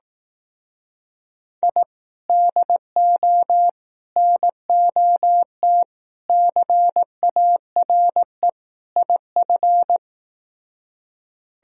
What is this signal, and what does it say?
Morse code, heard as a keyed tone, about 18 wpm, 700 Hz. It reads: I DO NOT CARE IF
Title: Speed-Racing - Sets of 5 Words 15wpm